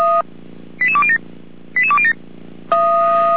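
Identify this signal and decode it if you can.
Five-tone selective call (ZVEI), heard as quick runs of five tones, tones of about 70 ms. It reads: 88207 88207
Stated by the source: ZVEI five-tone call 88124 received by SDR, logged by feuerwehr-orlamuende.org fire brigade server.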